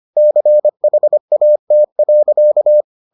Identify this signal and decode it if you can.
Morse code, heard as a keyed tone, about 25 wpm, 600 Hz. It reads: CHAT.